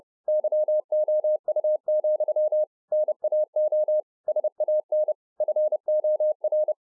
Morse code, heard as a keyed tone, about 30 wpm, 610 Hz. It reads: YOU, NAO SAN FOR